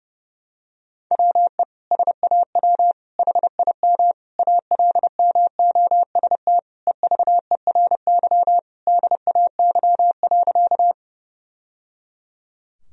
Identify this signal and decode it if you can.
Morse code, heard as a keyed tone, about 29 wpm, 700 Hz. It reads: WE SAW HIM ALMOST EVERY DAY.
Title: Sentences from Top 1000 Words - Review Part 4 - 30wpm